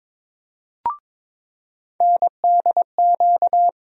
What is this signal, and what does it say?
Morse code, heard as a keyed tone, about 22 wpm, 700 Hz. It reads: NDQ
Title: USA Call Sign Postfixes - 3 Letters 22wpm